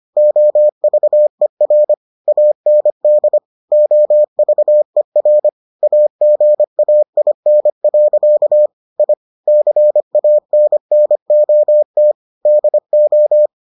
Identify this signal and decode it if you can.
Morse code, heard as a keyed tone, about 25 wpm, 600 Hz. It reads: OVER AND OVER AGAIN. I CANNOT DO